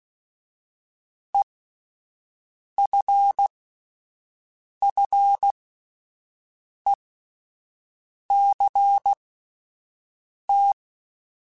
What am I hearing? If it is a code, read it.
Morse code, heard as a keyed tone, about 16 wpm, spaced as for 5 wpm, 800 Hz. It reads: EFFECT